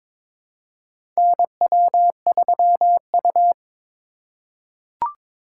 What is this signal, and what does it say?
Morse code, heard as a keyed tone, about 23 wpm, 700 Hz. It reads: NW3U
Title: USA Call Signs - 4 Characters - 2x Character Spacing - 22wpm